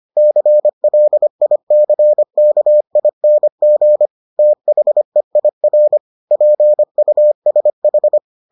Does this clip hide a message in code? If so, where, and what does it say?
Yes: Morse code from 0.2 to 8.2 s, heard as a keyed tone, about 25 wpm, 600 Hz. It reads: CLICKING THEIR PUSH